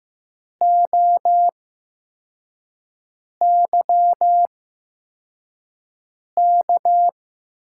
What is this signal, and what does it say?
Morse code, heard as a keyed tone, about 15 wpm, spaced as for 4 wpm, 700 Hz. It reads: OYK
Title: USA Call Signs - 5 Characters - 8x Character Spacing - 15wpm